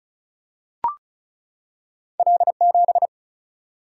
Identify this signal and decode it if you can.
Morse code, heard as a keyed tone, about 36 wpm, 700 Hz. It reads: L7
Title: Single-Letter Single-Number 35wpm